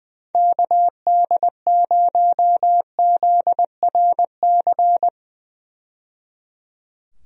Morse code, heard as a keyed tone, about 20 wpm, 700 Hz. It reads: KD0ZRC
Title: USA Call Signs - 6 Characters - 20wpm